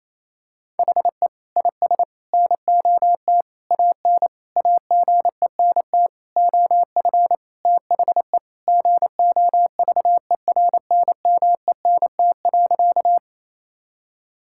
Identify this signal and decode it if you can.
Morse code, heard as a keyed tone, about 28 wpm, 700 Hz. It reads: HE IS NOT AN AGENT OF THE GOVERNMENT.
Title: Sentences from Top 1600 Words 28wpm